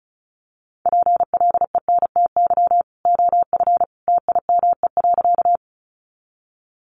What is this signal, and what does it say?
Morse code, heard as a keyed tone, about 34 wpm, 700 Hz. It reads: PLENTY OF TIME.
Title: Sentences from Top 1300 Words 35wpm